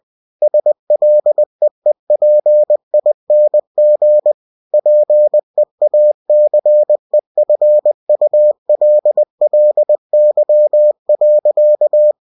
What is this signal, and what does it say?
Morse code, heard as a keyed tone, about 20 wpm, 600 Hz. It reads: SLEEPING PEACEFULLY.
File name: cw-books / winnie-the-pooh / winnie-20wpm-page-0085